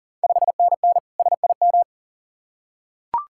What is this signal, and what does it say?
Morse code, heard as a keyed tone, about 40 wpm, 690 Hz. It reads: 5NN SIM
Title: Ontario QSO Party Exchange 40wpm